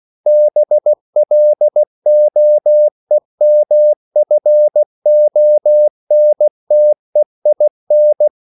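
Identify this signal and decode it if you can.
Morse code, heard as a keyed tone, about 16 wpm, 600 Hz. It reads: BLOEMFONTEIN
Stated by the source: CW practice of national capitals at 16 words per minute, from ON6ZQ.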